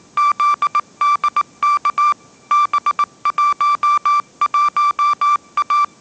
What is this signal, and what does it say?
Morse code, heard as a keyed tone, about 19 wpm, 1170 Hz. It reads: ZDK B11A
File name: mrx083 20wpm